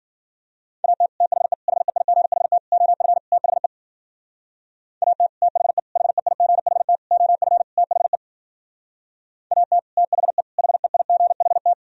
Morse code, heard as a keyed tone, about 49 wpm, 700 Hz. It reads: AT THE HEIGHT OF THE AT THE HEIGHT OF THE AT THE HEIGHT
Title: Speed-Racing - Sets of 5 Words 40wpm